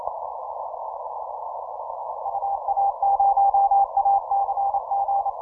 Morse code, heard as a keyed tone, about 29 wpm, 800 Hz. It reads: UA0ACG
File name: simple-file-list / TX0M OC297 40M CW